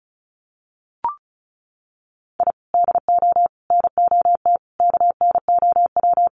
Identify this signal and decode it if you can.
Morse code, heard as a keyed tone, about 34 wpm, 700 Hz. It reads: I DO NOT KNOW
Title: Sets of 5 Words 35wpm